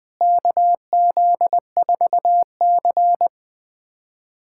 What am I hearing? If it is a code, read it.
Morse code, heard as a keyed tone, about 21 wpm, 700 Hz. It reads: KZ4C
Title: USA Callsign DE Callsign - 4 Characters - 20wpm